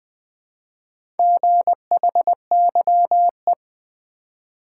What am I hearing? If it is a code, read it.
Morse code, heard as a keyed tone, about 20 wpm, 700 Hz. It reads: GHYE
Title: Four Letter-Number Combinations - 20wpm